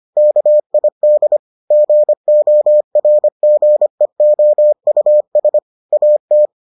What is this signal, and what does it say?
Morse code, heard as a keyed tone, about 25 wpm, 600 Hz. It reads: KID GORGEOUS AT